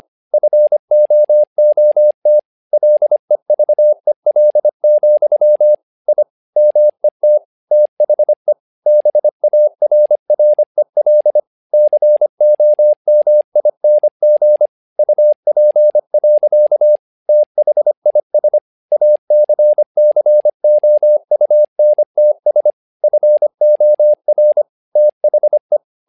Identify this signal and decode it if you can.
Morse code, heard as a keyed tone, about 25 wpm, 600 Hz. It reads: FOOT LEVEL, I MET THE BARREL COMING UP. THIS ACCOUNTS FOR THE
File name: learning center / tower trouble / tower trouble 25wpm